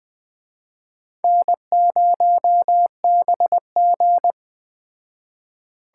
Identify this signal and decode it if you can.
Morse code, heard as a keyed tone, about 20 wpm, 700 Hz. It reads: N0BG
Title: Speed Racing - USA 1x2 Call Signs 17wpm